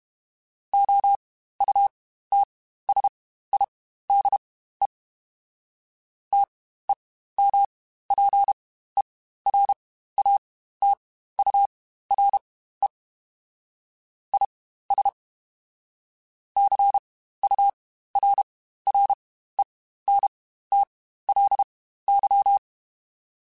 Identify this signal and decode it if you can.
Morse code, heard as a keyed tone, about 32 wpm, spaced as for 15 wpm, 800 Hz. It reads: OUTSIDE TEMPERATURE IS CURRENTLY
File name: Spaced Characters / Spaced Characters-mp3s / Weather 4 24 sp @30